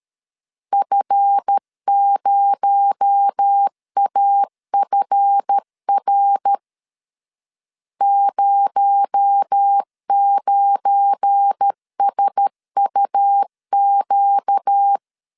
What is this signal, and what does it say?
Morse code, heard as a keyed tone, about 13 wpm, 800 Hz. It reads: F0AFR 09SUQ
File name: CW / DET / 1 9 1500C S C 12wpm 3811